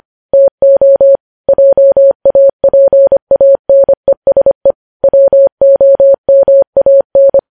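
Morse code, heard as a keyed tone, about 25 wpm, 550 Hz. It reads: TO JAPANESE WOMAN
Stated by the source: Morse news headlines sent at 25 wpm.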